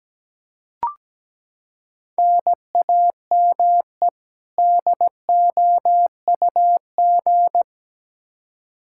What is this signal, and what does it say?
Morse code, heard as a keyed tone, about 17 wpm, 700 Hz. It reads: NAME DOUG
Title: Practice QSO - Exchange 1 - Beginning - 17wpm